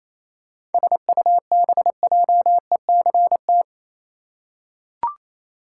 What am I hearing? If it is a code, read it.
Morse code, heard as a keyed tone, about 27 wpm, 700 Hz. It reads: SUBJECT
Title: Seven Letter Words - 28wpm